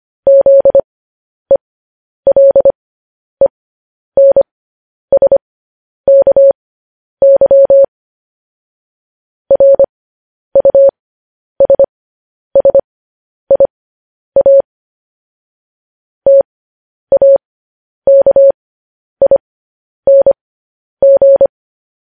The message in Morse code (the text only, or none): ZELENSKY RUSSIA TAKING